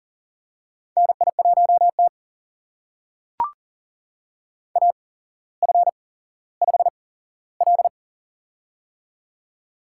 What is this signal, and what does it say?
Morse code, heard as a keyed tone, about 40 wpm, spaced as for 17 wpm, 700 Hz. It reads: NI1T AF5L
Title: USA Call Signs - 4 Characters - 8x Character Spacing - 40wpm